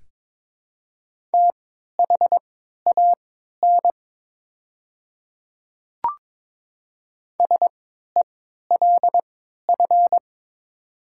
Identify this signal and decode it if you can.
Morse code, heard as a keyed tone, about 22 wpm, spaced as for 12 wpm, 700 Hz. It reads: THAN SELF
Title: 3x Character Spacing - Top 200 Words - 22wpm